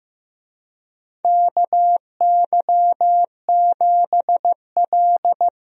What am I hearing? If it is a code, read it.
Morse code, heard as a keyed tone, about 16 wpm, 700 Hz. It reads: KY7L
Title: USA Call Signs - 4 Characters - Repeated 2x - 15wpm